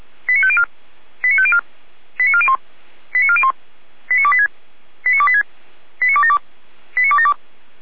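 Five-tone selective call (ZVEI), heard as quick runs of five tones, tones of about 70 ms. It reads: 89504 89504 89401 89401 89286 89286 89272 89272
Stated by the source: ZVEI five-tone selective call received on an SDR.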